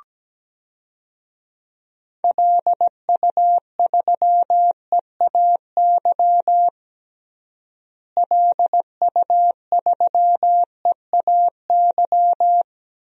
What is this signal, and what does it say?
Morse code, heard as a keyed tone, about 17 wpm, 700 Hz. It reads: LU3EAY LU3EAY
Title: DXCC-100 Call Signs - 6 Characters - Repeated 3x - 17wpm